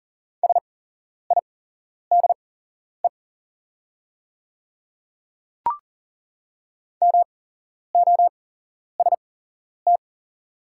Morse code, heard as a keyed tone, about 40 wpm, spaced as for 11 wpm, 700 Hz. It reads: SIDE MOST
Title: 8x Character Spacing - Top 100 Words - 40wpm